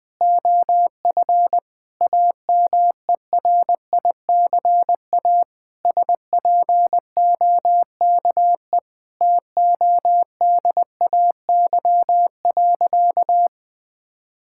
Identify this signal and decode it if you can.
Morse code, heard as a keyed tone, about 20 wpm, 700 Hz. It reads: OF AMERICA SPOKE TODAY.
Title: Sentences with 8-Word Phrases - 20wpm